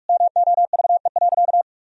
Morse code, heard as a keyed tone, about 42 wpm, 700 Hz. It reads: MOVE.
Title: Sentences from Top 1000 Words - Review Part 1 - 45wpm